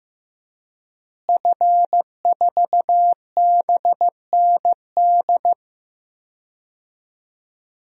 Morse code, heard as a keyed tone, about 15 wpm, 700 Hz. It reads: F4BND
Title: DXCC-100 Call Signs - 5 Characters - Repeated 3x - 15wpm